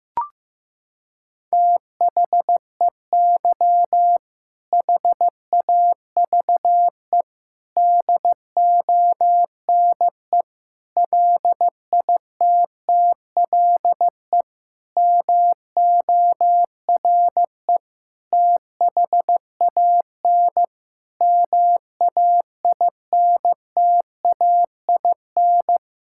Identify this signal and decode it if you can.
Morse code, heard as a keyed tone, about 16 wpm, 700 Hz. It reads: THEY HAVE DONE LITTLE MORE THAN MAINTAIN
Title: Sentences from Top 1000 Words - Review Part 5 - 15wpm